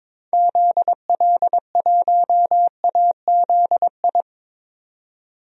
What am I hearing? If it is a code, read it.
Morse code, heard as a keyed tone, about 22 wpm, 700 Hz. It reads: ZL1AZI
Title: DXCC-100 Call Signs - 6 Characters - 22wpm